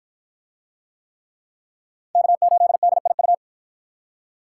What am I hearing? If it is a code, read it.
Morse code, heard as a keyed tone, about 50 wpm, 700 Hz. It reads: K8DIU